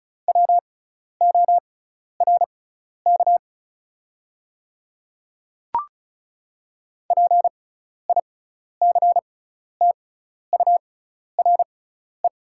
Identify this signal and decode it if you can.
Morse code, heard as a keyed tone, about 34 wpm, spaced as for 12 wpm, 700 Hz. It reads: WORK PICTURE
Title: 6x Character Spacing - Top 200 Words - 35wpm